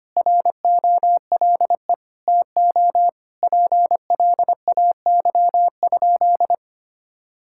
Morse code, heard as a keyed tone, about 25 wpm, 700 Hz. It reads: ROLE TO PLAY?
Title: Sentences from Top 1000 Words - Review Part 6 - 25wpm